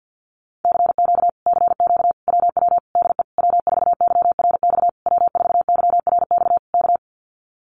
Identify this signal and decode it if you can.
Morse code, heard as a keyed tone, about 50 wpm, 700 Hz. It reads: CQ CQ WW DE W4YRX W4YRX K
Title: Common QSO Phrases Encore 50wpm